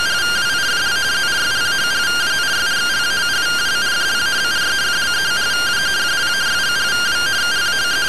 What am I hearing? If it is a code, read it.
Two-tone teleprinter signal, about 170 Hz shift.